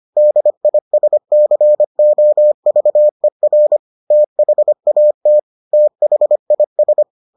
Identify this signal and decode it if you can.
Morse code, heard as a keyed tone, about 25 wpm, 600 Hz. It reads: DISCOVER THAT THIS